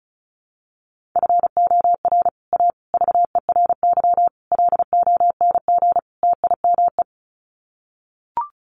Morse code, heard as a keyed tone, about 34 wpm, 700 Hz. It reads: FOR A VERY LONG TIME
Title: Sets of 5 Words 35wpm